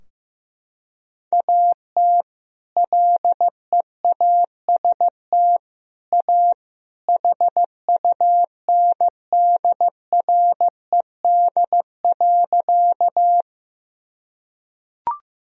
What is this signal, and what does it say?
Morse code, heard as a keyed tone, about 15 wpm, 700 Hz. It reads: AT LEAST A HUNDRED.
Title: Sentences from Top 1000 Words - Review Part 2 - 15wpm